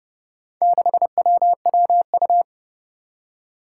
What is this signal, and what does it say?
Morse code, heard as a keyed tone, about 30 wpm, 700 Hz. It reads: 6WWU
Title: Four Challenging Letter-Number Combinations 30wpm